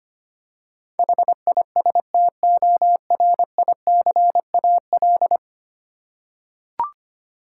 Morse code, heard as a keyed tone, about 25 wpm, 700 Hz. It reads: HISTORICAL
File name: Ten Letter Words - 25wpm